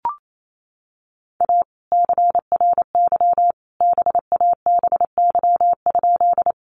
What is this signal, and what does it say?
Morse code, heard as a keyed tone, about 28 wpm, 700 Hz. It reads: A CRY BABY?
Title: Sentences from Top 1300 Words 28wpm